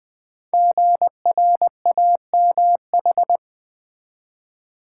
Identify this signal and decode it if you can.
Morse code, heard as a keyed tone, about 20 wpm, 700 Hz. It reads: GRAMH
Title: Code Groups - 5 Letters - 20wpm